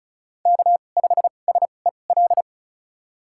Morse code, heard as a keyed tone, about 34 wpm, spaced as for 25 wpm, 690 Hz. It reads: K5SEL
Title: USA Call Signs - 5 Characters - 2x Character Spacing - 35wpm